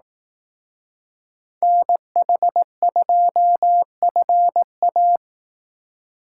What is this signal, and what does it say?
Morse code, heard as a keyed tone, about 18 wpm, 700 Hz. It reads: NH2FA
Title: Speed Racing - USA Call Signs 15wpm